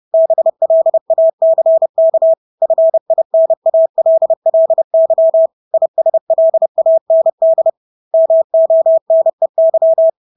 Morse code, heard as a keyed tone, about 30 wpm, 650 Hz. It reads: BLACK FINALLY ISLAND MONEY
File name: Engelske ord 30-30 wpm